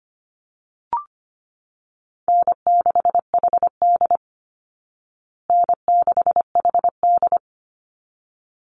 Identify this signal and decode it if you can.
Morse code, heard as a keyed tone, about 25 wpm, 700 Hz. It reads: N6HD N6HD
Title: Call Signs Repeated 3x - 25wpm